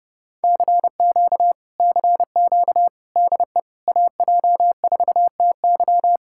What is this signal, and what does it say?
Morse code, heard as a keyed tone, about 31 wpm, 700 Hz. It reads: CQ CQ DE AJ4TY